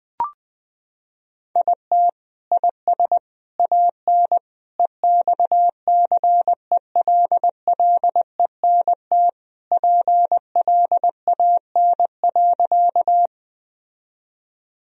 Morse code, heard as a keyed tone, about 20 wpm, 700 Hz. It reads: IT IS AN EXCELLENT PLAN.
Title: Sentences from Top 1000 Words - Review Part 2 - 20wpm